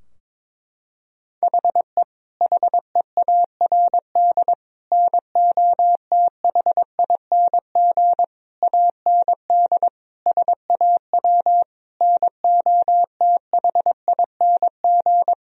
Morse code, heard as a keyed tone, about 22 wpm, 700 Hz. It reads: HE HEARD NOTHING AND SAW NOTHING
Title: Sentences from Top 1000 Words - Review Part 3 - 22wpm